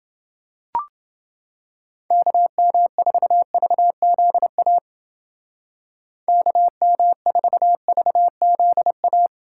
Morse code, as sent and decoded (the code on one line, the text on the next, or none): -.- -- ....- ...- --.. .- / -.- -- ....- ...- --.. .-
KM4VZA KM4VZA